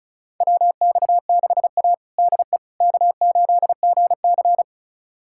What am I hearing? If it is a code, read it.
Morse code, heard as a keyed tone, about 36 wpm, 700 Hz. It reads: WX6A DE K8GC
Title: USA Callsign DE Callsign - 4 Characters - 35wpm